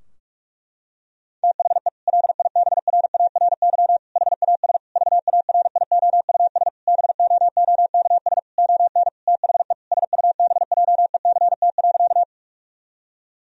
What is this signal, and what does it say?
Morse code, heard as a keyed tone, about 46 wpm, 700 Hz. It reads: THE LIBRARY HAS VARIOUS BOOKS ON THE SUBJECT.
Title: Sentences from Top 1000 Words - Review Part 4 - 45wpm